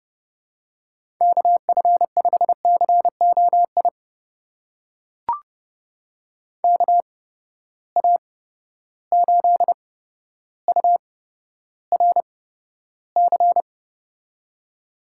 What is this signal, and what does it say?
Morse code, heard as a keyed tone, about 30 wpm, spaced as for 13 wpm, 700 Hz. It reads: KF5COI KA8URC
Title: USA Call Signs - 6 Characters - 8x Character Spacing - 30wpm